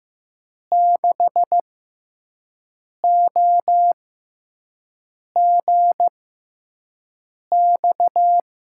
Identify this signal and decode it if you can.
Morse code, heard as a keyed tone, about 15 wpm, spaced as for 5 wpm, 700 Hz. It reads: 6OGX
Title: USA Call Signs - 6 Characters - 6x Character Spacing - 15wpm